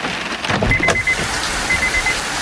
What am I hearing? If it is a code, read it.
There is no code.